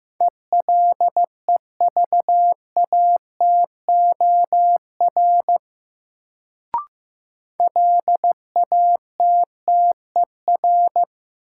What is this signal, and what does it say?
Morse code, heard as a keyed tone, about 15 wpm, 700 Hz. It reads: ELEVATOR LATTER
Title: Sentences from Top 1600 Words - New Word Drill - 15wpm